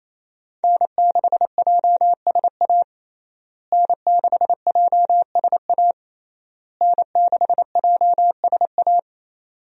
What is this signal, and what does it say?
Morse code, heard as a keyed tone, about 27 wpm, 700 Hz. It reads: N6JSA N6JSA N6JSA
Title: USA Call Signs - 5 Characters - Repeated 3x - 28wpm